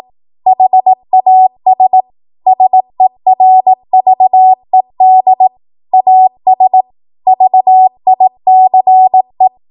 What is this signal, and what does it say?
Morse code, heard as a keyed tone, about 18 wpm, 750 Hz. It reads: HAS SERVED AS VICE
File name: Archive / 18 WPM / 18WPM